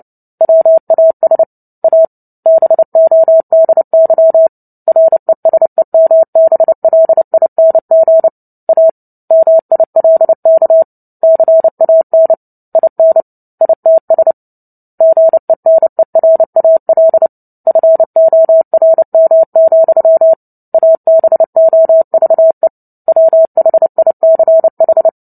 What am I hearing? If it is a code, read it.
Morse code, heard as a keyed tone, about 30 wpm, 650 Hz. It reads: WAS A BODY RESEMBLING A MILK CAN IN ITS GENERAL FORM, ABOVE WHICH
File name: Book Two / 03-Chapter 3 - 29 WPM